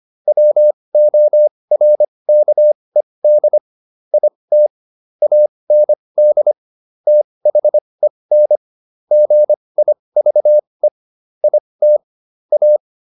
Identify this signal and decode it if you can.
Morse code, heard as a keyed tone, about 25 wpm, spaced as for 20 wpm, 600 Hz. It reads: WORKED IT AND THEN GIVE IT A